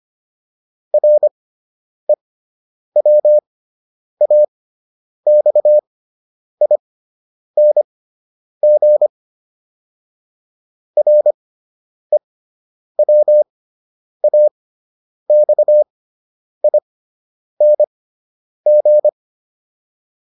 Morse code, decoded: REWAXING REWAXING

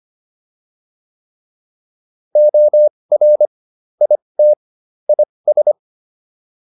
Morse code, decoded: OR IT IS

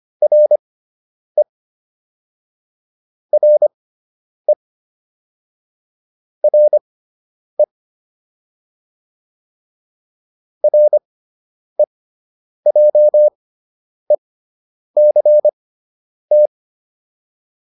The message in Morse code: RE RE RE REJECT

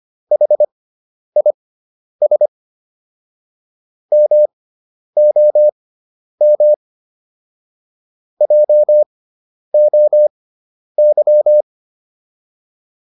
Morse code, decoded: HIS MOM JOY

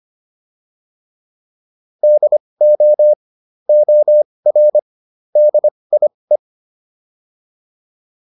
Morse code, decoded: DO OR DIE

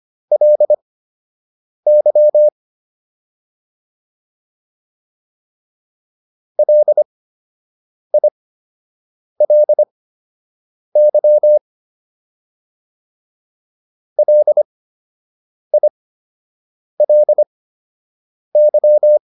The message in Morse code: LY LILY LILY

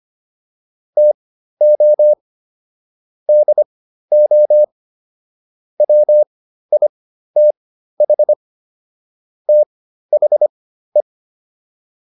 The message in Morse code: TO DO WITH THE

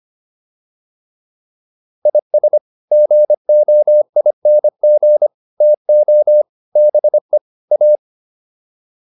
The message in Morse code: IS GOING TO BE A

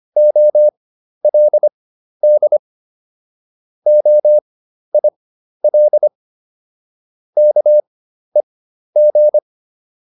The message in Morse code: OLD OIL KEG

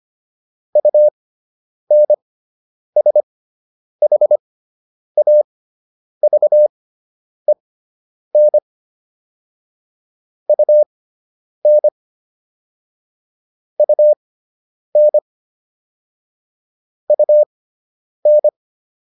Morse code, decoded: UNSHAVEN UN UN UN

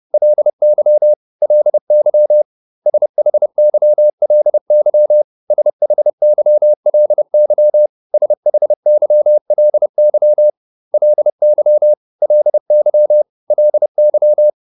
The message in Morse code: LY LY SHYLY SHYLY SHYLY LY LY LY

